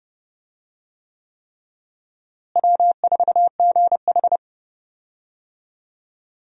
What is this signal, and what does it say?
Morse code, heard as a keyed tone, about 32 wpm, 700 Hz. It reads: W4GH
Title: USA Call Signs - 4 Characters - Repeated 3x - 30wpm